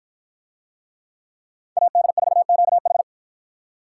Morse code, heard as a keyed tone, about 49 wpm, 700 Hz. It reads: AD4QH